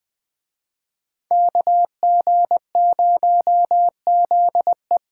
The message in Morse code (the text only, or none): KG0ZE